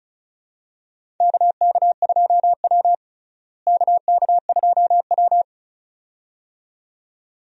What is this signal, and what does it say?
Morse code, heard as a keyed tone, about 34 wpm, 700 Hz. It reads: KK2W KK2W
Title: USA Call Signs - 4 Characters - Repeated 2x - 35wpm